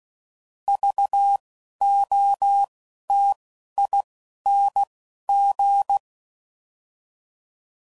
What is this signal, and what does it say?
Morse code, heard as a keyed tone, about 16 wpm, spaced as for 11 wpm, 800 Hz. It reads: VOTING
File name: Prefixes / By First Letter / Vs